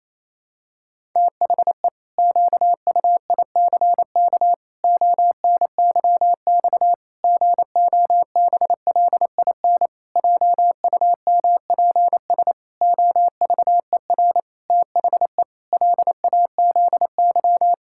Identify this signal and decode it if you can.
Morse code, heard as a keyed tone, about 28 wpm, 700 Hz. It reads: THE QUICK ONYX GOBLIN JUMPS OVER THE LAZY
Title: Pangram Challenge v2 - 28wpm